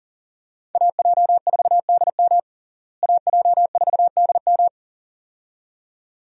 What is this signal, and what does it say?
Morse code, heard as a keyed tone, about 40 wpm, 700 Hz. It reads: AJ4DM AJ4DM
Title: USA Call Signs - 5 Characters - Repeated 2x - 40wpm